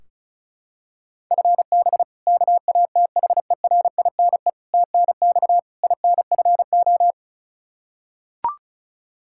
Morse code, heard as a keyed tone, about 36 wpm, 700 Hz. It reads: FB KATHERINE TNX INFO
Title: Practice QSO - Exchange 3 - More Info - 35wpm